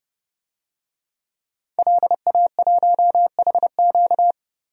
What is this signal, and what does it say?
Morse code, heard as a keyed tone, about 30 wpm, 700 Hz. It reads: LA1HQ
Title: DXCC-100 Call Signs - 5 Characters - Repeated 3x - 30wpm